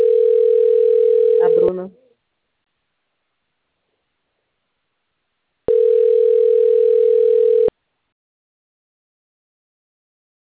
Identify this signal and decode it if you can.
Ringback tone (North America), 440 + 480 Hz, with speech.
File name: out- - - - - .27023